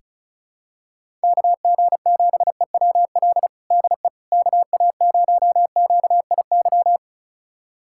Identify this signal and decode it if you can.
Morse code, heard as a keyed tone, about 36 wpm, 700 Hz. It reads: KG7EWL DE KA0QIY
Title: USA Callsign DE Callsign - 6 Characters - 35wpm